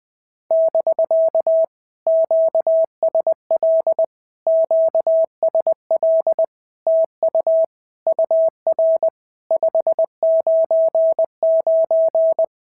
Morse code, heard as a keyed tone, about 20 wpm, 650 Hz. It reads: <BK> QSL QSL TU UR 599
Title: POTA QSO Practice - 20wpm